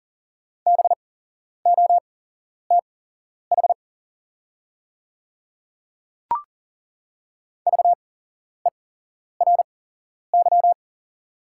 Morse code, heard as a keyed tone, about 40 wpm, spaced as for 11 wpm, 690 Hz. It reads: BOTH VERY
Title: 8x Character Spacing - Top 500 Words - 40wpm